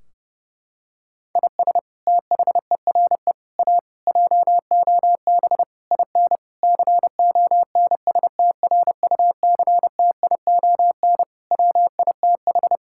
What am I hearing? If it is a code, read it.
Morse code, heard as a keyed tone, about 30 wpm, 700 Hz. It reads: IS THERE A JOB IN CONSTRUCTION WITH
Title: Sentences from Top 1600 Words 30wpm